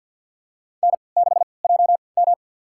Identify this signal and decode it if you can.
Morse code, heard as a keyed tone, about 52 wpm, spaced as for 28 wpm, 700 Hz. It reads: N6JM
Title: USA Call Signs - 4 Characters - 3x Character Spacing - 50wpm